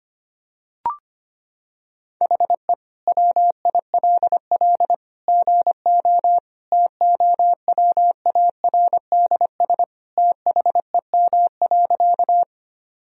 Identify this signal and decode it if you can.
Morse code, heard as a keyed tone, about 25 wpm, 700 Hz. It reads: HE WILL GO TOWARDS THEM.
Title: Sentences from Top 1000 Words - Review Part 2 - 25wpm